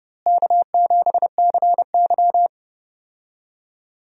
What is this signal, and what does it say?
Morse code, heard as a keyed tone, about 30 wpm, 700 Hz. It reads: K7CY
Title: USA Call Signs - 4 Characters - Repeated 3x - 30wpm